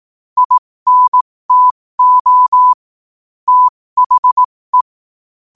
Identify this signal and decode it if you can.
Morse code, heard as a keyed tone, about 18 wpm, spaced as for 15 wpm, 1000 Hz. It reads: INTO THE
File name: qotdfiles / 15wpm